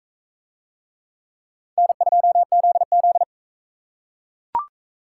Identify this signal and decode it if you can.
Morse code, heard as a keyed tone, about 37 wpm, 700 Hz. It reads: N2ZZ